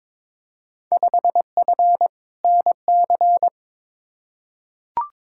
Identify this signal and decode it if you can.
Morse code, heard as a keyed tone, about 22 wpm, 700 Hz. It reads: 5F NC
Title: Field Day Exchanges Encore - 22wpm